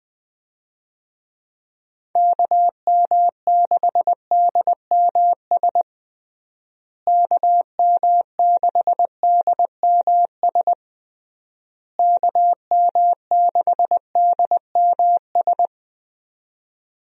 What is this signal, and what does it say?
Morse code, heard as a keyed tone, about 21 wpm, 700 Hz. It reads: KM6DMS KM6DMS KM6DMS
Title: USA Call Signs - 6 Characters - Repeated 3x - 20wpm